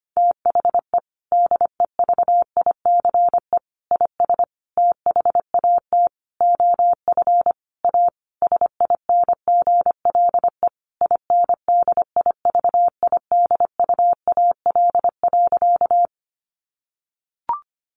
Morse code, heard as a keyed tone, about 25 wpm, 700 Hz. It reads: THE DEVICE IS THAT OF A SINGLE INDIVIDUAL.
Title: Sentences from Top 1500 Words 25wpm